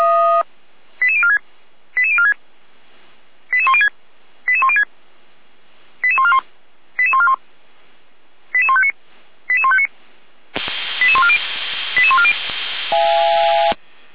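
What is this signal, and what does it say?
Five-tone selective call (ZVEI), heard as quick runs of five tones, tones of about 70 ms. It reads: 88046 88046 88106 88106 88141 88141 88159 88159 88140 88140
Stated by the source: ZVEI five-tone selective call received on an SDR, from the Feuerwehr Orlamünde fire brigade audio server.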